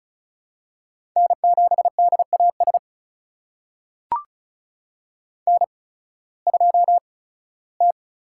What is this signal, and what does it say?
Morse code, heard as a keyed tone, about 36 wpm, spaced as for 19 wpm, 700 Hz. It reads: N7DAS N2T
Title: USA Call Signs - 8x Character Spacing - 35wpm